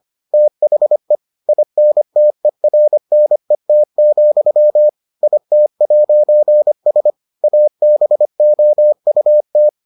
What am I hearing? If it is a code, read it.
Morse code, heard as a keyed tone, about 25 wpm, 600 Hz. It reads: THE INTERNET, IT'S ABOUT